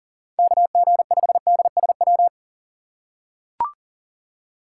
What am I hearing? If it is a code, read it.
Morse code, heard as a keyed tone, about 40 wpm, 700 Hz. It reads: KG5DSW
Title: USA Call Signs - 6 Characters - 6x Character Spacing - 40wpm